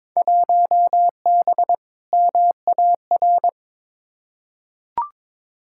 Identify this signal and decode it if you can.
Morse code, heard as a keyed tone, about 22 wpm, 700 Hz. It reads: 1B MAR
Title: Field Day Exchanges Encore - 22wpm